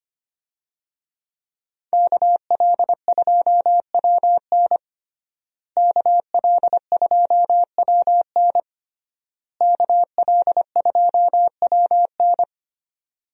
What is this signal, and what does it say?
Morse code, heard as a keyed tone, about 25 wpm, 700 Hz. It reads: KL2WN KL2WN KL2WN
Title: USA Call Signs - 5 Characters - Repeated 3x - 25wpm